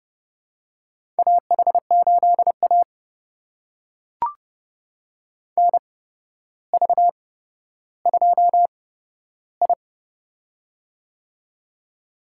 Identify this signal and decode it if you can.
Morse code, heard as a keyed tone, about 30 wpm, spaced as for 13 wpm, 700 Hz. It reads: AH8A NV2I
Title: USA Call Signs - 4 Characters - 8x Character Spacing - 30wpm